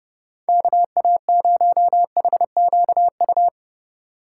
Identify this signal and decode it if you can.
Morse code, heard as a keyed tone, about 37 wpm, 700 Hz. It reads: KA0HQU
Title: Speed Racing - USA 2x3 Call Signs 30wpm